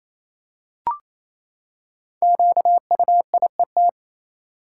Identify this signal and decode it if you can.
Morse code, heard as a keyed tone, about 28 wpm, 700 Hz. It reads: QUIET
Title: Five Letter Words - 28wpm